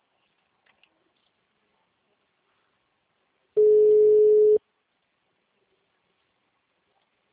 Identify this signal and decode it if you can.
Ringback tone (Europe, 425 Hz).
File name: out- -2014- - - .98